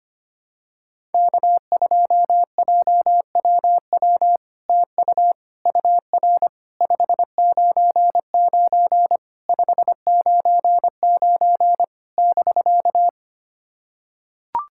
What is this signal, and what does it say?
Morse code, heard as a keyed tone, about 25 wpm, 700 Hz. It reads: K2JWW TU UR 599 599 <BK>
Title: SOTA QSO Practice 25wpm